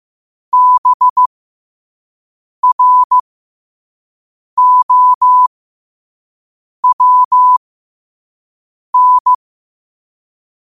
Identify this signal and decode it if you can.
Morse code, heard as a keyed tone, about 15 wpm, spaced as for 5 wpm, 1000 Hz. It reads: BROWN
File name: qotdfiles / 5wpm